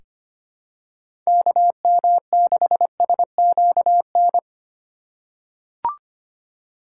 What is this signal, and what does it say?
Morse code, heard as a keyed tone, about 25 wpm, 700 Hz. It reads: KM6SQN
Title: USA Call Signs - 4x Character Spacing - 25wpm